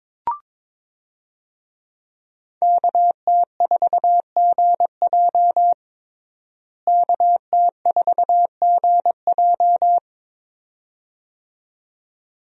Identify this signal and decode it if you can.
Morse code, heard as a keyed tone, about 22 wpm, 700 Hz. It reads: KT4GJ KT4GJ
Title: USA Call Signs - 5 Characters - Repeated 2x - 22wpm